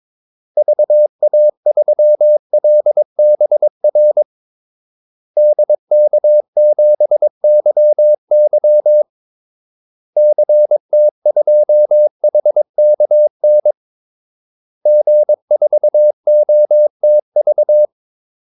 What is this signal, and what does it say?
Morse code, heard as a keyed tone, about 22 wpm, 600 Hz. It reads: VA3LBR DK7YY CT2HKN G4OTV
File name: Callsigns-22wpm